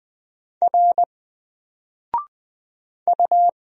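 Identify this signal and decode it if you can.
Morse code, heard as a keyed tone, about 20 wpm, 700 Hz. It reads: R U
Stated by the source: Single letter and number Morse drill at 20 wpm, from N4LCD.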